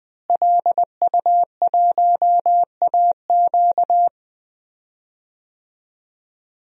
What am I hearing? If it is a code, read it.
Morse code, heard as a keyed tone, about 20 wpm, 700 Hz. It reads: LU1AQ
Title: DXCC-100 Call Signs - 5 Characters - Repeated 2x - 20wpm